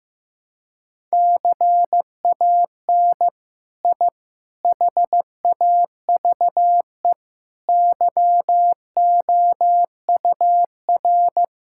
Morse code, heard as a keyed tone, about 15 wpm, 700 Hz. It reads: CAN I HAVE YOUR
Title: Sentences from Top 1000 Words - Review Part 5 - 15wpm